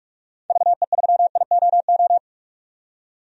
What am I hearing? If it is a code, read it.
Morse code, heard as a keyed tone, about 46 wpm, 700 Hz. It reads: VE3IOO
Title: DXCC-100 Call Signs - 6 Characters - 45wpm